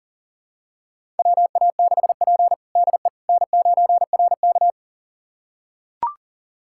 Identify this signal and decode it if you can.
Morse code, heard as a keyed tone, about 37 wpm, 700 Hz. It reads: WA6P DE N9RK
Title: USA Callsign DE Callsign - 4 Characters - 40wpm